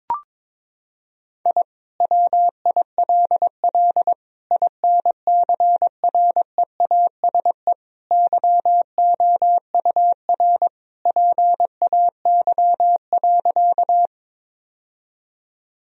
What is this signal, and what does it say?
Morse code, heard as a keyed tone, about 22 wpm, 700 Hz. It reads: I WILL INCREASE YOUR PAY.
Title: Sentences from Top 1000 Words - Review Part 4 - 22wpm